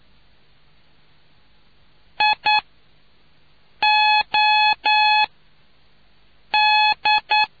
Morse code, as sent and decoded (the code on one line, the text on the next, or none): .. / --- / -..
I O D